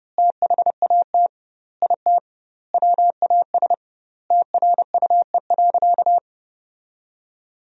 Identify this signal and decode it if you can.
Morse code, heard as a keyed tone, about 30 wpm, 700 Hz. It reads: THAT IT WAS TRUE.
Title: 2x Word-Spacing - Sentences from Top 400 Words - 30wpm